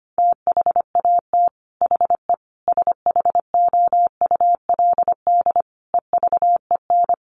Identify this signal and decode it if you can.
Morse code, heard as a keyed tone, about 25 wpm, 700 Hz. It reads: THAT HE SHOULD EVEN
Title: Sentences from Top 1500 Words 25wpm